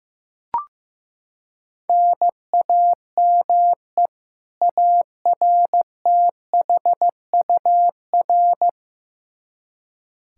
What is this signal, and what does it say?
Morse code, heard as a keyed tone, about 15 wpm, 700 Hz. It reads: NAME ARTHUR
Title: Practice QSO - Exchange 1 - Beginning - 15wpm